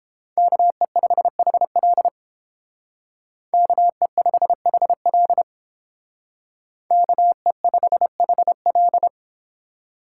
Morse code, heard as a keyed tone, about 31 wpm, 700 Hz. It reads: KE5HL KE5HL KE5HL